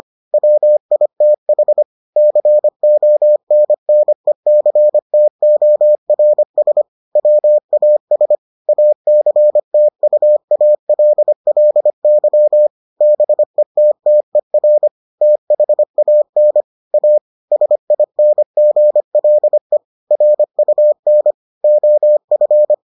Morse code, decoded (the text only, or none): WITH CONNECTORS WAS ACTUALLY BETTER THAN A SINGLE RUN OF